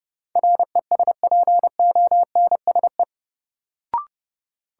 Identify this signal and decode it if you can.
Morse code, heard as a keyed tone, about 30 wpm, 700 Hz. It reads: RESPONSE
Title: Sentences from Top 1400 Words - New Word Drill - 30wpm